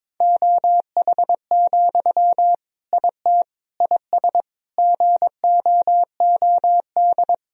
Morse code, heard as a keyed tone, about 22 wpm, 700 Hz. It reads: OH, IT IS GOOD
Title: Sentences from Top 1000 Words - Review Part 3 - 22wpm